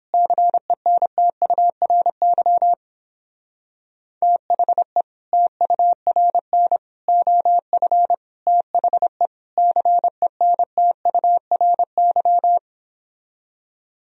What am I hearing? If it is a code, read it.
Morse code, heard as a keyed tone, about 29 wpm, 700 Hz. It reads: CENTURY THE TURN OF THE CENTURY